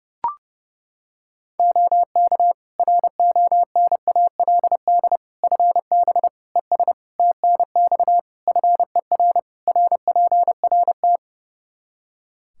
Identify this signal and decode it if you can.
Morse code, heard as a keyed tone, about 30 wpm, 700 Hz. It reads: OK RONALD FB ES TNX FER RPRT
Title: Practice QSO - Exchange 2 - Info - 30wpm